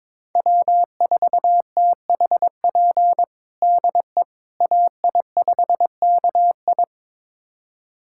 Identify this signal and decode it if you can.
Morse code, heard as a keyed tone, about 21 wpm, 700 Hz. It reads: W4THP DE AI5KI
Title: USA Callsign DE Callsign - 5 Characters - 22wpm